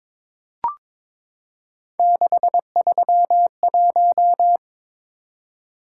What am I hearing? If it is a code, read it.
Morse code, heard as a keyed tone, about 22 wpm, 700 Hz. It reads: 631
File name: practice / 22wpm / Three Letter-Number Combinations - 22wpm